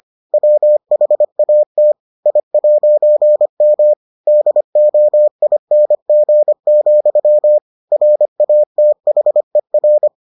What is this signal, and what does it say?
Morse code, heard as a keyed tone, about 25 wpm, 600 Hz. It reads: WHAT I'M DOING, RATHER